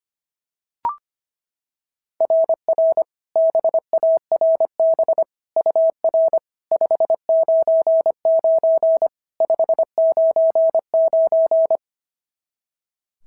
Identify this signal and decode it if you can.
Morse code, heard as a keyed tone, about 25 wpm, 690 Hz. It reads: RR BARB UR 599 599
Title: SOTA QSO Practice 25wpm